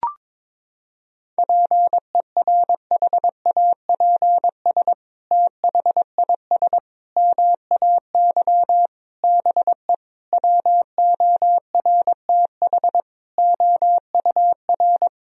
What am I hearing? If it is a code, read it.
Morse code, heard as a keyed tone, about 22 wpm, 700 Hz. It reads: PERHAPS THIS MAY BE WORTH OUR
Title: Sentences from Top 1000 Words - Review Part 2 - 22wpm